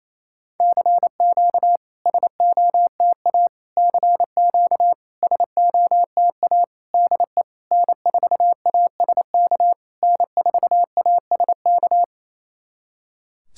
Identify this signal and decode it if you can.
Morse code, heard as a keyed tone, about 27 wpm, 690 Hz. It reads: CQ SOTA CQ SOTA DE N4ASK N4ASK
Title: SOTA QSO Practice 28wpm